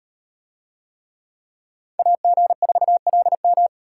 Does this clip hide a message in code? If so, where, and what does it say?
Yes: Morse code from 2.0 to 3.7 s, heard as a keyed tone, about 38 wpm, 700 Hz. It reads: AG4LM